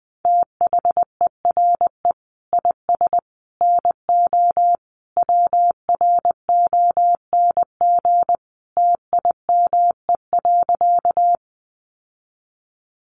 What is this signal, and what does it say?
Morse code, heard as a keyed tone, about 20 wpm, 700 Hz. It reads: THERE IS NO WRONG TIME.
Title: Sentences from Top 900 Words 20wpm